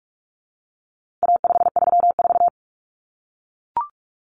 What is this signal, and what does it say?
Morse code, heard as a keyed tone, about 44 wpm, 700 Hz. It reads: A534
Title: Common Cut Numbers - 4-Digit 45wpm